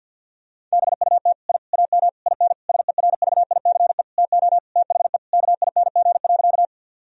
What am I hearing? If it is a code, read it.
Morse code, heard as a keyed tone, about 50 wpm, 700 Hz. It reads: BUT I AM IN SERVICE TO THE KING.
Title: Sentences from Top 1000 Words - Review Part 4 - 50wpm